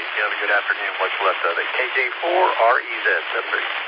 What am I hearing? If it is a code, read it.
Emergency Alert System (SAME) tones, heard as harsh buzzing bursts, with speech. It is heard from 0.3 to 3.3 s.